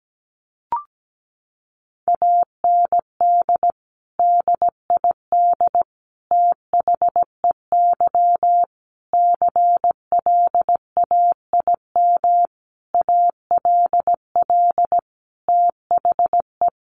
Morse code, heard as a keyed tone, about 17 wpm, 700 Hz. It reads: AND DID THEY CLAIM ALL THE
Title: Sentences from Top 1500 Words 17wpm